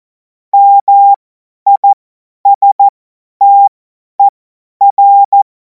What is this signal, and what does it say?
Morse code, heard as a keyed tone, about 14 wpm, spaced as for 10 wpm, 800 Hz. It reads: MISTER
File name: Morsekurs / Witze / Witz 12. Mai 2026 50 F70